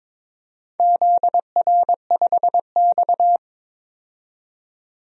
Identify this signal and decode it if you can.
Morse code, heard as a keyed tone, about 22 wpm, 700 Hz. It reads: ZR5X
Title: DXCC-100 Call Signs - 4 Characters - 22wpm